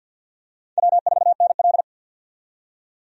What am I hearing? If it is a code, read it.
Morse code, heard as a keyed tone, about 53 wpm, 700 Hz. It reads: W4NL
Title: USA Call Signs - 4 Characters - 50wpm